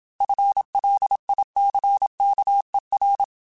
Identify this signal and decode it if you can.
Morse code, heard as a keyed tone, about 27 wpm, 800 Hz. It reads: FLICKER
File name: FL Runs / FL Runs mp3s / FL Runs @25